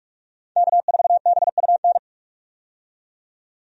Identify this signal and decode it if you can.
Morse code, heard as a keyed tone, about 45 wpm, 700 Hz. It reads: K4BUN
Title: USA Call Signs - 5 Characters - 45wpm